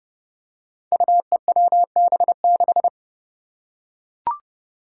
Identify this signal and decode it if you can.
Morse code, heard as a keyed tone, about 28 wpm, 700 Hz. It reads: UEWB6
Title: Five Challenging Letter-Number Combinations 30wpm